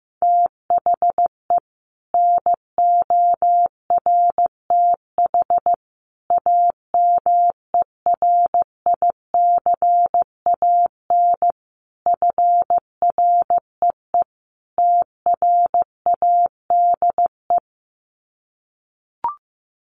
Morse code, heard as a keyed tone, about 15 wpm, 700 Hz. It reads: THE NORTH AMERICAN FREE TRADE
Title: Sets of 5 Words Encore 15wpm